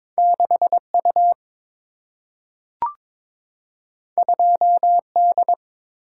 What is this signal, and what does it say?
Morse code, heard as a keyed tone, about 22 wpm, 700 Hz. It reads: 6 U 2 D
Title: Single-Letter Single-Number 22wpm